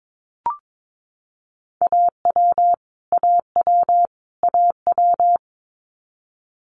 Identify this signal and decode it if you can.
Morse code, heard as a keyed tone, about 22 wpm, 700 Hz. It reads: AW AW AW
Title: Multi-letter Phonemes Repeated 3x 22wpm